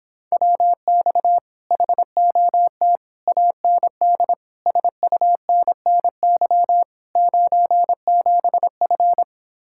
Morse code, heard as a keyed tone, about 26 wpm, 700 Hz. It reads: WX HOT AND SUNNY 97F